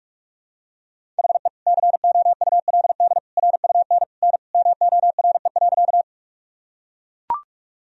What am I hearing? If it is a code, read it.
Morse code, heard as a keyed tone, about 43 wpm, 700 Hz. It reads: HE COULD RUN N MORE.
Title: Sentences from Top 300 Words and Abbreviations 45wpm